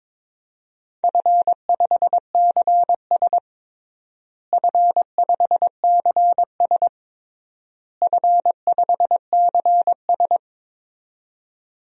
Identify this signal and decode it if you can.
Morse code, heard as a keyed tone, about 21 wpm, 700 Hz. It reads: F5CS F5CS F5CS
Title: DXCC-100 Call Signs - 4 Characters - Repeated 3x - 22wpm